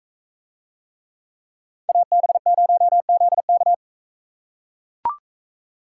Morse code, heard as a keyed tone, about 43 wpm, 700 Hz. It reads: AB0ZK